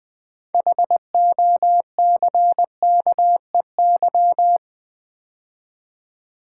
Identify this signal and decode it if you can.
Morse code, heard as a keyed tone, about 20 wpm, 700 Hz. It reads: HOCKEY
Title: 6 Letter Words - First 3 Letters Given - 20wpm